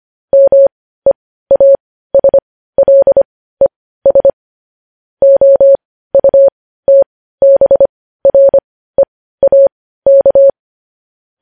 Morse code, heard as a keyed tone, about 25 wpm, spaced as for 15 wpm, 550 Hz. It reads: MEASLES OUTBREAK
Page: news-headlines-in-morse-code-at-15-wpm